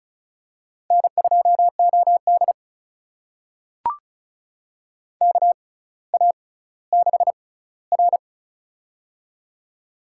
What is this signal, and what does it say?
Morse code, heard as a keyed tone, about 36 wpm, spaced as for 18 wpm, 700 Hz. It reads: N2OD KA6R
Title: USA Call Signs - 6x Character Spacing - 35wpm